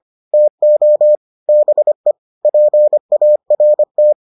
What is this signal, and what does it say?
Morse code, heard as a keyed tone, about 25 wpm, 600 Hz. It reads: TO BE PART